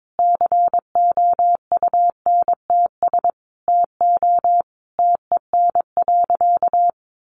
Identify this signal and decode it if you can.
Morse code, heard as a keyed tone, about 22 wpm, 700 Hz. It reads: COUNTS TO TEN.
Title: Easy Phrases 22wpm